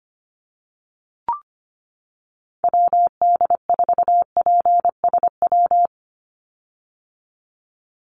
Morse code, heard as a keyed tone, about 25 wpm, 700 Hz. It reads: WD4PSW
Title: USA Call Signs 25wpm